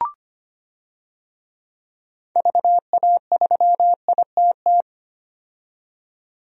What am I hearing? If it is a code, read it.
Morse code, heard as a keyed tone, about 25 wpm, 700 Hz. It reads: VA3ITT
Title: DXCC-100 Call Signs - 6 Characters - 25wpm